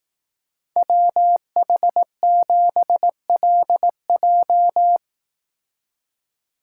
Morse code, heard as a keyed tone, about 17 wpm, 700 Hz. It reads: WH7LJ
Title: Speed Racing - USA 2x2 Call Signs 15wpm